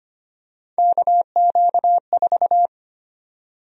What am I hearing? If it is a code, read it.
Morse code, heard as a keyed tone, about 25 wpm, 700 Hz. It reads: KQ4
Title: USA Call Sign Prefixes - 25wpm